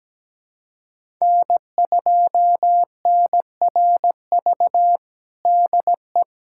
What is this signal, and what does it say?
Morse code, heard as a keyed tone, about 17 wpm, 700 Hz. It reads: N2NRV DE
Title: USA Callsign DE Callsign - 5 Characters - 17wpm